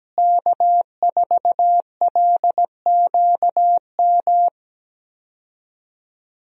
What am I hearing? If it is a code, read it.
Morse code, heard as a keyed tone, about 17 wpm, 700 Hz. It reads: K4LQM
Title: USA Call Signs - 5 Characters - Repeated 3x - 17wpm